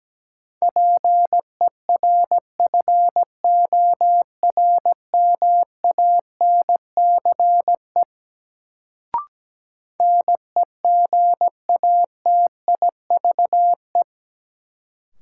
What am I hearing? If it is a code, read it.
Morse code, heard as a keyed tone, about 17 wpm, 700 Hz. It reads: PERFORMANCE NEGATIVE
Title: Sentences from Top 1000 Words - New Word Drill - 17wpm